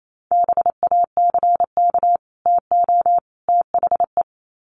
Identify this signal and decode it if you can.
Morse code, heard as a keyed tone, about 28 wpm, 700 Hz. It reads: BACK TO THE
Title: Sets of 3 Words 28wpm